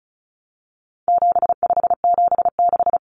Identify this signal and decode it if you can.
Morse code, heard as a keyed tone, about 34 wpm, 700 Hz. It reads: 7576
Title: Common Cut Numbers - 4-Digit 35wpm